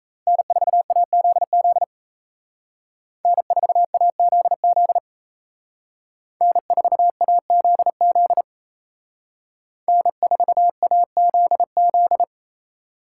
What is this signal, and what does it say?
Morse code, heard as a keyed tone, about 33 wpm, 700 Hz. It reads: N4AZZ N4AZZ N4AZZ N4AZZ
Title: Speed Racing - USA 1x3 Call Signs 28wpm